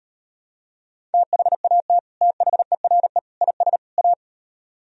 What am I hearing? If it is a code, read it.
Morse code, heard as a keyed tone, about 38 wpm, 700 Hz. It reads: THAT THERE IS A